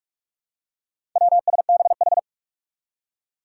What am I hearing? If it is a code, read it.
Morse code, heard as a keyed tone, about 44 wpm, 700 Hz. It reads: WSBH
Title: Four Challenging Letter-Number Combinations 45wpm